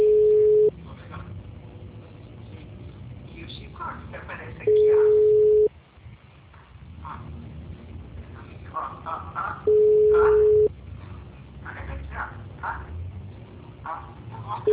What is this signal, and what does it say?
Ringback tone (Europe, 425 Hz), with speech.
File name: out- -2018- - - .29761